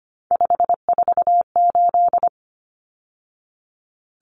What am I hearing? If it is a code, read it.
Morse code, heard as a keyed tone, about 25 wpm, 700 Hz. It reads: 548
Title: Numbers - 3-Digit 25wpm